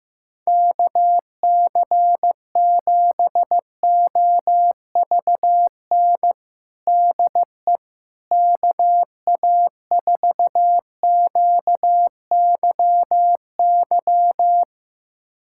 Morse code, heard as a keyed tone, about 16 wpm, 700 Hz. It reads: KC7OVN DE KA4QYY
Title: USA Callsign DE Callsign 15wpm